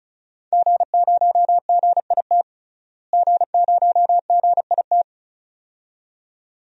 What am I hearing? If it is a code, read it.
Morse code, heard as a keyed tone, about 35 wpm, 700 Hz. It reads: G0GIT G0GIT
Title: DXCC-100 Call Signs - 5 Characters - Repeated 2x - 35wpm